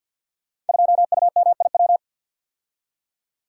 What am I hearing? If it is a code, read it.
Morse code, heard as a keyed tone, about 48 wpm, 700 Hz. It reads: 2UMIW
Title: Code Groups - 5 Letter:Numbers - 50wpm